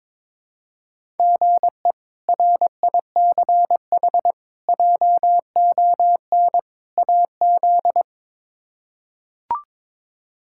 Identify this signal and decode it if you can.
Morse code, heard as a keyed tone, about 22 wpm, 700 Hz. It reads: GE RICH JON AZ
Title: K1USN SST QSO Practice - 22wpm